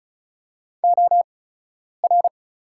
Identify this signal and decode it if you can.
Morse code, heard as a keyed tone, about 36 wpm, spaced as for 10 wpm, 700 Hz. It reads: OR
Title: 8x Character Spacing - Top 100 Words - 35wpm